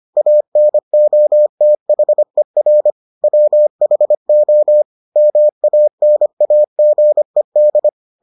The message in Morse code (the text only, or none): ANOTHER WHO MANAGED